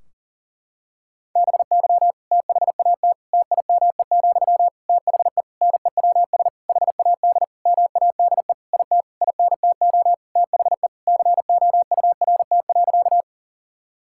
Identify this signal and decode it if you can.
Morse code, heard as a keyed tone, about 37 wpm, 700 Hz. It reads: BY THAT TIME, THE NEWS HAD MADE IT INTO THE COURT.
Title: Sentences from Top 1600 Words 40wpm